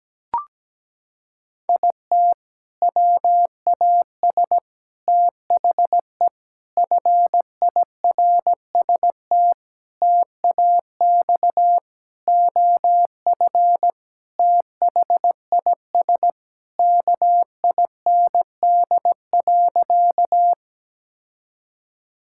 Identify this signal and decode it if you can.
Morse code, heard as a keyed tone, about 17 wpm, 700 Hz. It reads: IT WAS THE FIRST TAX OF THIS KIND.
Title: Sentences from Top 1000 Words - Review Part 5 - 17wpm